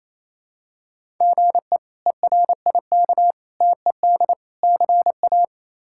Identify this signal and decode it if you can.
Morse code, heard as a keyed tone, about 28 wpm, 700 Hz. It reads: GE ERIK TED CA
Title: K1USN SST QSO Practice - 28wpm